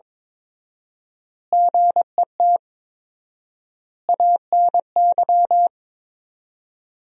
Morse code, heard as a keyed tone, about 21 wpm, 700 Hz. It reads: GET ANY
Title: 4x Word-Spacing - Sentences from Top 400 Words - 22wpm